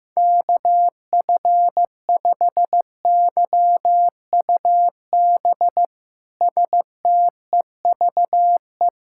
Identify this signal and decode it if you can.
Morse code, heard as a keyed tone, about 16 wpm, 700 Hz. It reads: KF5YUB STEVE
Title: K1USN SST QSO Practice - 15wpm